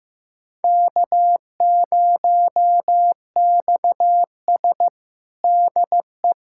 Morse code, heard as a keyed tone, about 15 wpm, 700 Hz. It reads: K0XS DE
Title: USA Callsign DE Callsign - 4 Characters - 15wpm